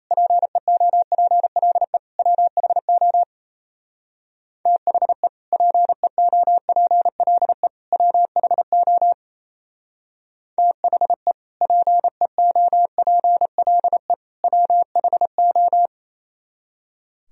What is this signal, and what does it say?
Morse code, heard as a keyed tone, about 37 wpm, 700 Hz. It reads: PEOPLE WHO THE PEOPLE WHO THE PEOPLE WHO